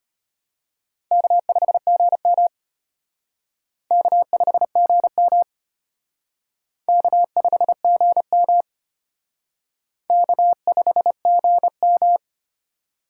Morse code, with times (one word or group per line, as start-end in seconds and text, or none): 1.1-2.5 K5GM
3.9-5.4 K5GM
6.9-8.6 K5GM
10.1-12.2 K5GM